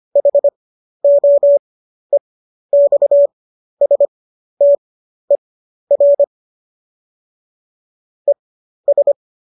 Morse code, heard as a keyed tone, about 25 wpm, spaced as for 12 wpm, 570 Hz. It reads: HOEXSTER ES
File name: qso012 12